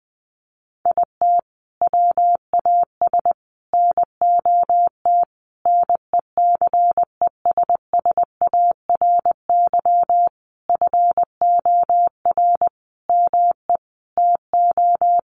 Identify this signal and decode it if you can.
Morse code, heard as a keyed tone, about 20 wpm, 700 Hz. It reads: IT WAS NOT NECESSARY FOR ME TO
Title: Sentences from Top 1300 Words 20wpm